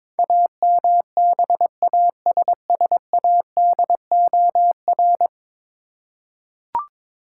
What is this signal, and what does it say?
Morse code, heard as a keyed tone, about 22 wpm, 700 Hz. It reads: AMBASSADOR
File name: practice / 22wpm / 10 Letter Words - First 3 Letters Given - 22wpm